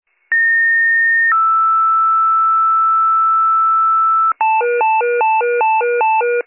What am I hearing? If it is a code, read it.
Two-tone page, heard as a one-second tone, then a three-second tone, about 1850 then 1320 Hz, with noise. It is heard from 0.3 to 4.3 s.